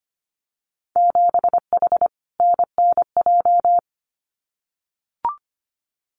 Morse code, heard as a keyed tone, about 25 wpm, 700 Hz. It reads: 7H NNJ
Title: Winter Field Day Exchanges Encore 25wpm